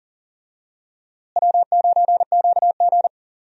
Morse code, heard as a keyed tone, about 40 wpm, 700 Hz. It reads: W9QG
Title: USA Call Signs - 4 Characters - Repeated 2x - 40wpm